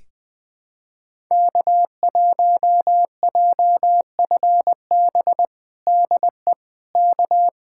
Morse code, heard as a keyed tone, about 21 wpm, 700 Hz. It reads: K1JFB DE K
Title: USA Callsign DE Callsign 20wpm